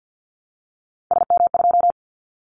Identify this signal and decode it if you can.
Morse code, heard as a keyed tone, about 50 wpm, 700 Hz. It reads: SM2
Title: Callsign Prefixes for Top 25 DXCC - Repeated 3x - 50wpm